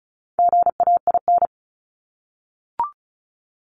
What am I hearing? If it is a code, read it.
Morse code, heard as a keyed tone, about 34 wpm, 700 Hz. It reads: GAIN
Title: Top 751-1000 Words 35wpm